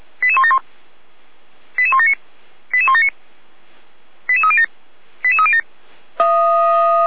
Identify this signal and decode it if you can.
Five-tone selective call (ZVEI), heard as quick runs of five tones, tones of about 70 ms. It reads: 88171 88179 88179 88307 88307
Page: /455/SDR/ZVEI/88352/2025.01.30/